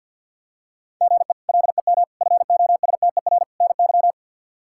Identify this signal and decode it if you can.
Morse code, heard as a keyed tone, about 46 wpm, 700 Hz. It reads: GE LEM FOSTER NY
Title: K1USN SST QSO Practice - 50wpm